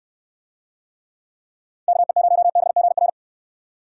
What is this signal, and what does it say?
Morse code, heard as a keyed tone, about 62 wpm, 700 Hz. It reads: KE0BGR